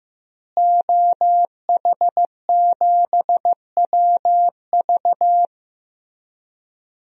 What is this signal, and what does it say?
Morse code, heard as a keyed tone, about 15 wpm, 700 Hz. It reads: OH7WV
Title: DXCC-100 Call Signs - 5 Characters - Repeated 3x - 15wpm